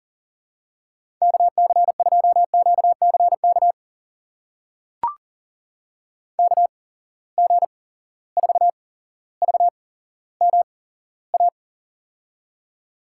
Morse code, heard as a keyed tone, about 37 wpm, spaced as for 17 wpm, 700 Hz. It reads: KC2QCK KG4VMA
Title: USA Call Signs - 6 Characters - 8x Character Spacing - 40wpm